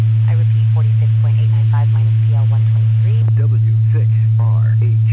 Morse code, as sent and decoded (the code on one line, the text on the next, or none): none